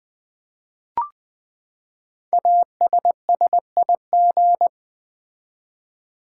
Morse code, heard as a keyed tone, about 20 wpm, 700 Hz. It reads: ASSIG
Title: Build A Word - 5 Letters At A Time - 10-Letter Words - 20wpm